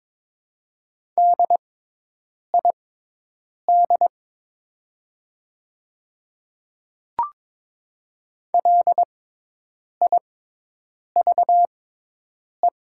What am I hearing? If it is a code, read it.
Morse code, heard as a keyed tone, about 22 wpm, spaced as for 8 wpm, 700 Hz. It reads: DID LIVE
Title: 6x Character Spacing - Top 300 Words - 22wpm